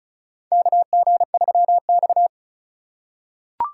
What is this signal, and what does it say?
Morse code, heard as a keyed tone, about 36 wpm, 700 Hz. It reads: KG3X
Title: USA Call Signs - 4 Characters - 6x Character Spacing - 35wpm